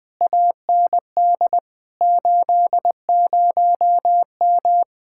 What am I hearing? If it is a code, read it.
Morse code, heard as a keyed tone, about 20 wpm, 700 Hz. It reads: AND 80M
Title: Speed-Racing Common QSO Phrases - 17wpm